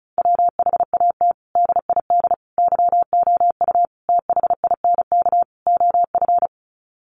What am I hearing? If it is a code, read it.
Morse code, heard as a keyed tone, about 34 wpm, 700 Hz. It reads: WHAT DID YOU THINK OF
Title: Sets of 5 Words Encore2 35wpm